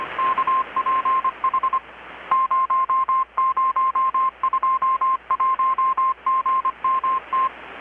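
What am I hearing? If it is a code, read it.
Morse code, heard as a keyed tone, about 25 wpm, 1050 Hz. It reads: KPH 0021GMT
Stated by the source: Morse from KPH's Night of Nights XX maritime radiotelegraphy event, received in Chiapas, Mexico about 3,800 km away on a Kenwood R-600 with a grounded 10 m long-wire antenna.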